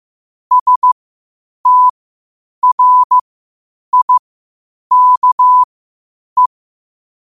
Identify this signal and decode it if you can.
Morse code, heard as a keyed tone, about 15 wpm, spaced as for 8 wpm, 1000 Hz. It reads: STRIKE
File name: qotdfiles / 8wpm